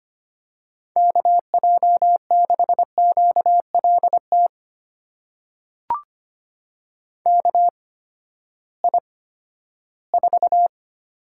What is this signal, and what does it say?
Morse code, heard as a keyed tone, about 25 wpm, spaced as for 14 wpm, 700 Hz. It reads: KJ6QLT KI4
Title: USA Call Signs - 6 Characters - 8x Character Spacing - 25wpm